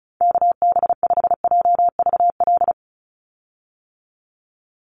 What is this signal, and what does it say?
Morse code, heard as a keyed tone, about 34 wpm, 700 Hz. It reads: KB5JVL
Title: USA Call Signs 35wpm